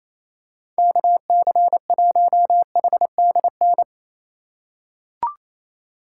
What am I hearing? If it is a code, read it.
Morse code, heard as a keyed tone, about 28 wpm, 700 Hz. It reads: KC1HDN
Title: USA Call Signs - 6 Characters - 4x Character Spacing - 28wpm